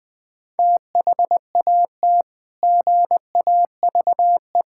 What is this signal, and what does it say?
Morse code, heard as a keyed tone, about 21 wpm, 700 Hz. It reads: THAT GAVE
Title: Sentences from Top 1000 Words - Review Part 6 - 20wpm